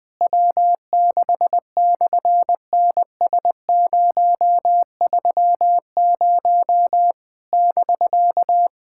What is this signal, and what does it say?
Morse code, heard as a keyed tone, about 19 wpm, 700 Hz. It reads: W6/NS030 <BK>
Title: SOTA QSO Practice 20wpm